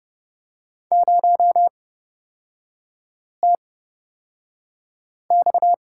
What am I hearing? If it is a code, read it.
Morse code, heard as a keyed tone, about 29 wpm, spaced as for 5 wpm, 700 Hz. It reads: 0TX
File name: Lesson 37e - Letter x - Three Characters 30wpm